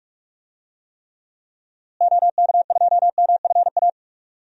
Morse code, heard as a keyed tone, about 45 wpm, 700 Hz. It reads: OK2MFA
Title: DXCC-100 Call Signs - 6 Characters - Repeated 3x - 45wpm